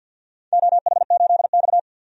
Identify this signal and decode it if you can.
Morse code, heard as a keyed tone, about 51 wpm, 700 Hz. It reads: OH8X
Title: DXCC-100 Call Signs - 4 Characters - 50wpm